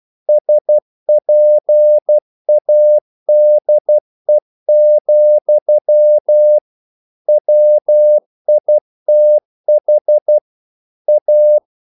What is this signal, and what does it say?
Morse code, heard as a keyed tone, about 12 wpm, 600 Hz. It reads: SPADE, WITH A
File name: mp3 files / the gardener - 12wpm